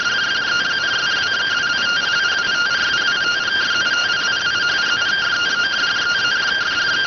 Two-tone teleprinter signal, about 170 Hz shift.